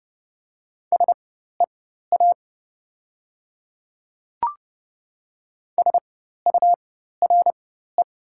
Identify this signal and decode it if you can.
Morse code, heard as a keyed tone, about 30 wpm, spaced as for 14 wpm, 700 Hz. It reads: SEA SURE
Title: 4x Character Spacing - Top 300 Words - 30wpm